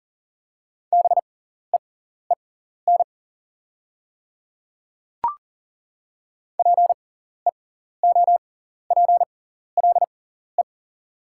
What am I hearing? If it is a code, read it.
Morse code, heard as a keyed tone, about 37 wpm, spaced as for 14 wpm, 700 Hz. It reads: BEEN PEOPLE
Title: 6x Character Spacing - Top 100 Words - 40wpm